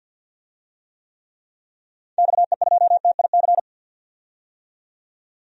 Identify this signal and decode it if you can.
Morse code, heard as a keyed tone, about 53 wpm, 700 Hz. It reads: XE2TIC
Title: DXCC-100 Call Signs - 6 Characters - 50wpm